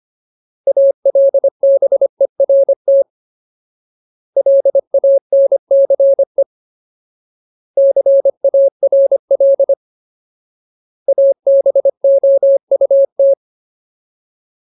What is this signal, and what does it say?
Morse code, heard as a keyed tone, about 25 wpm, 550 Hz. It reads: ALBERT LANCE CARL ABOUT
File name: Ultimate Word List / 25WPM Ultimate